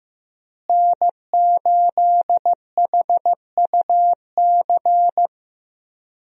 Morse code, heard as a keyed tone, about 15 wpm, 700 Hz. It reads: N8HUC
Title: USA Call Signs - 5 Characters - Repeated 3x - 15wpm